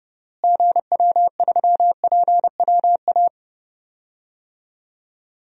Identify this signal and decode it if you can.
Morse code, heard as a keyed tone, about 30 wpm, 700 Hz. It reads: GW3PWA
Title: DXCC-100 Call Signs - 6 Characters - Repeated 3x - 30wpm